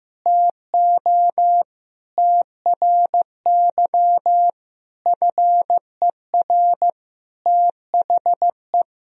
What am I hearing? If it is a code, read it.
Morse code, heard as a keyed tone, about 15 wpm, 700 Hz. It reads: TO TRY FER THE
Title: Sentences from Top 400 Words and Abbreviations 15wpm